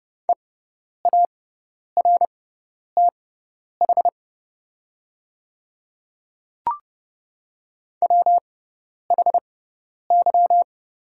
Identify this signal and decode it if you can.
Morse code, heard as a keyed tone, about 30 wpm, spaced as for 10 wpm, 700 Hz. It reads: EARTH WHY
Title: 6x Character Spacing - Top 500 Words - 30wpm